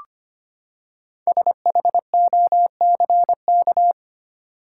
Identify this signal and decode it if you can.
Morse code, heard as a keyed tone, about 25 wpm, 700 Hz. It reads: SHOCK
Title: Five Letter Words - 25wpm